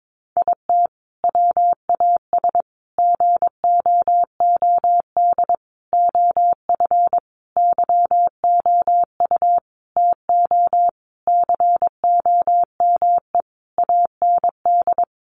Morse code, heard as a keyed tone, about 22 wpm, 700 Hz. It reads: IT WAS GOOD OF YOU TO COME AND
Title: Sentences from Top 1100 Words 22wpm